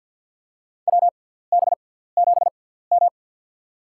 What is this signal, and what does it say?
Morse code, heard as a keyed tone, about 52 wpm, spaced as for 17 wpm, 700 Hz. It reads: WB7M
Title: USA Call Signs - 4 Characters - 6x Character Spacing - 50wpm